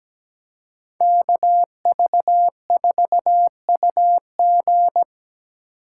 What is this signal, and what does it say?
Morse code, heard as a keyed tone, about 20 wpm, 700 Hz. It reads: KV4UG
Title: Speed Racing - USA 2x2 Call Signs 17wpm